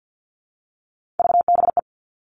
Morse code, heard as a keyed tone, about 50 wpm, 700 Hz. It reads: VBE
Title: Three Challenging Letter-Number Combinations 50wpm